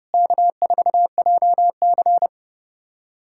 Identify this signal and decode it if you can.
Morse code, heard as a keyed tone, about 30 wpm, 700 Hz. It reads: K4JC
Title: USA Callsign DE Callsign - 4 Characters - 30wpm